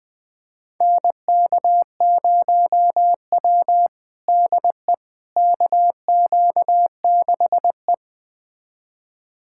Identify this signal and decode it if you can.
Morse code, heard as a keyed tone, about 20 wpm, 700 Hz. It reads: NK0W DE KQ6E
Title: USA Callsign DE Callsign - 4 Characters - 20wpm